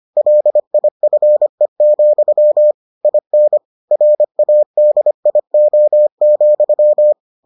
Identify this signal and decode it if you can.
Morse code, heard as a keyed tone, about 25 wpm, 600 Hz. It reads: LIFE, IN RADIO,